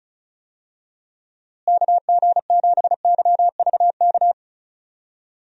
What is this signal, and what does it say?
Morse code, heard as a keyed tone, about 36 wpm, 700 Hz. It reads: KG7YVK
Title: USA Call Signs - 6 Characters - Repeated 2x - 35wpm